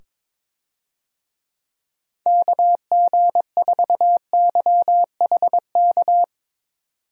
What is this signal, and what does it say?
Morse code, heard as a keyed tone, about 21 wpm, 700 Hz. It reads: KG4YHK
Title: USA Call Signs - 6 Characters - 22wpm